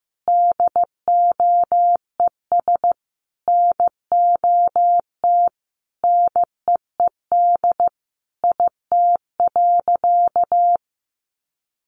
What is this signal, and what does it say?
Morse code, heard as a keyed tone, about 15 wpm, 700 Hz. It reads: DOES NOT NEED IT.
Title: Sentences from Top 1400 Words 15wpm